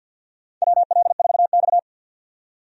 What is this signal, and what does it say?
Morse code, heard as a keyed tone, about 53 wpm, 700 Hz. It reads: WL4X
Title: USA Call Signs - 4 Characters - 50wpm